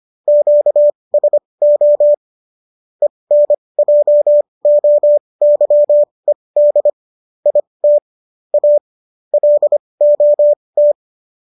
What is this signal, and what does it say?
Morse code, heard as a keyed tone, about 25 wpm, spaced as for 20 wpm, 590 Hz. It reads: QSO ENJOYED IT A LOT